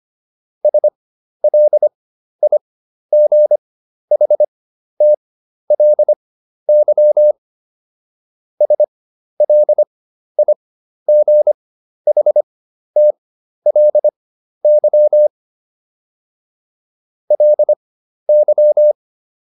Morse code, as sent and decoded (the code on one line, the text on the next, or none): ... .-.. .. --. .... - .-.. -.-- / ... .-.. .. --. .... - .-.. -.-- / .-.. -.--
SLIGHTLY SLIGHTLY LY